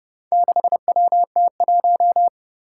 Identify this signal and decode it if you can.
Morse code, heard as a keyed tone, about 30 wpm, 700 Hz. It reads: 6WT1
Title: Four Challenging Letter-Number Combinations 30wpm